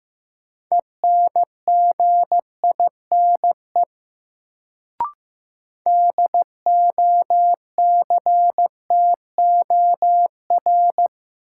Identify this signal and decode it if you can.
Morse code, heard as a keyed tone, about 15 wpm, 700 Hz. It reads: ENGINE DOCTOR
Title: Sentences from Top 1500 Words - New Word Drill - 15wpm